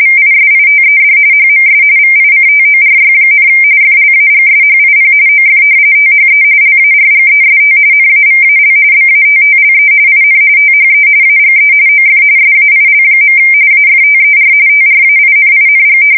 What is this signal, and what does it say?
RTTY, about 45.45 baud. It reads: VISITED HMS VICTORY, FLAGSHIP OF THE LARGEST NAVAL BATTLE IN THE AGE OF SAIL. AFTER TRAINING
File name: fsk170 / battleshipnj